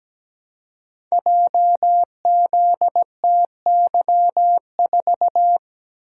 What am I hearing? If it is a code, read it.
Morse code, heard as a keyed tone, about 17 wpm, 700 Hz. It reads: JZTY4
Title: Code Groups - 5 Letter:Numbers - 17wpm